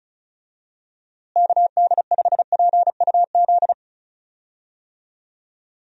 Morse code, heard as a keyed tone, about 34 wpm, 700 Hz. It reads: KD5PUZ
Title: USA Call Signs - 6 Characters - Repeated 2x - 35wpm